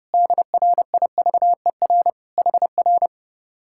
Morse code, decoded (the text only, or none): DRIVER HR